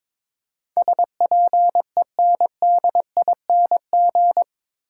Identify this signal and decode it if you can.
Morse code, heard as a keyed tone, about 22 wpm, 700 Hz. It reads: SPENDING
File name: practice / 22wpm / Eight Letter Words - 22wpm